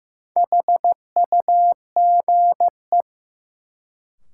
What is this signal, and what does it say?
Morse code, heard as a keyed tone, about 15 wpm, 700 Hz. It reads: HUGE
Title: Sentences from Top 1100 Words - New Word Drill Alt - 15wpm